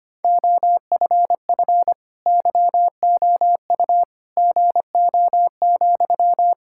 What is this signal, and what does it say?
Morse code, heard as a keyed tone, about 25 wpm, 700 Hz. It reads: OFF YOU GO,
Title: Sentences from Top 1000 Words - Review Part 6 - 25wpm